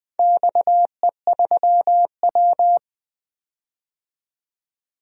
Morse code, heard as a keyed tone, about 20 wpm, 700 Hz. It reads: XE3W
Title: DXCC-100 Call Signs - 4 Characters - 20wpm